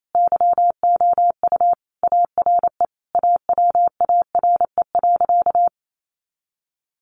Morse code, heard as a keyed tone, about 28 wpm, 700 Hz. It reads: YOU ARE AWARE.
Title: Sentences from Top 1300 Words 28wpm